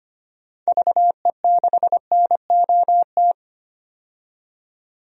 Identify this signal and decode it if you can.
Morse code, heard as a keyed tone, about 25 wpm, 690 Hz. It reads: VE6NOT
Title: DXCC-100 Call Signs - 6 Characters - 25wpm